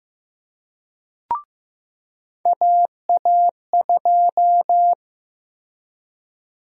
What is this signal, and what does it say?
Morse code, heard as a keyed tone, about 16 wpm, 700 Hz. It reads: AA2
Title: USA Call Sign Prefixes - Repeated 2x - 15wpm